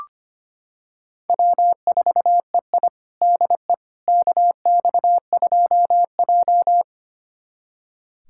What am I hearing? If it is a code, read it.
Morse code, heard as a keyed tone, about 25 wpm, 700 Hz. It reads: W4EI DE KX2J
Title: USA Callsign DE Callsign - 4 Characters - 25wpm